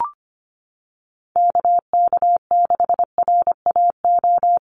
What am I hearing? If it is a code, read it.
Morse code, heard as a keyed tone, about 25 wpm, 700 Hz. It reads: KK6RAO
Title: USA Call Signs Repeated 3x 25wpm